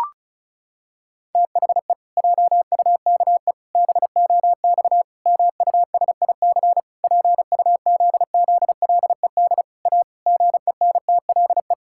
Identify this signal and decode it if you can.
Morse code, heard as a keyed tone, about 36 wpm, 700 Hz. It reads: THE JUKE BOX MUSIC PUZZLED A GENTLE
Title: Pangram Challenge v2 - 35wpm